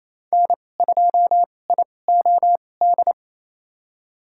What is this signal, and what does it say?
Morse code, heard as a keyed tone, about 28 wpm, spaced as for 20 wpm, 700 Hz. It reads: N2IOD
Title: USA Call Signs - 5 Characters - 2x Character Spacing - 28wpm